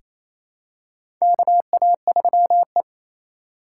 Morse code, heard as a keyed tone, about 28 wpm, 700 Hz. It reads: KA3E
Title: USA Call Signs - 6x Character Spacing - 28wpm